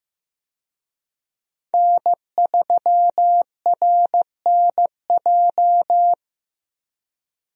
Morse code, heard as a keyed tone, about 15 wpm, 700 Hz. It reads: N3RNJ
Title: USA Call Signs - 5 Characters - 15wpm